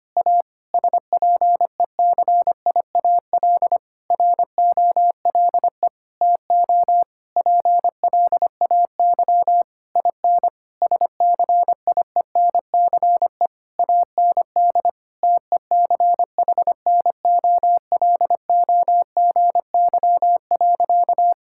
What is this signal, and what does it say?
Morse code, heard as a keyed tone, about 25 wpm, 700 Hz. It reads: A SPECIAL ROLE TO PLAY IN SCIENCE AND TECHNOLOGY.
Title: Sentences from Top 1000 Words - Review Part 6 - 25wpm